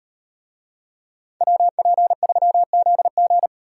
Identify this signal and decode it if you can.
Morse code, heard as a keyed tone, about 35 wpm, 700 Hz. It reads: WP3ZG